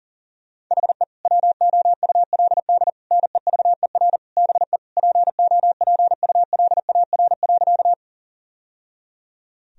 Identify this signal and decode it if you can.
Morse code, heard as a keyed tone, about 37 wpm, 700 Hz. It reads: HE WOULD NEVER BE POPULAR.
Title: Sentences from Top 1000 Words - Review Part 3 - 40wpm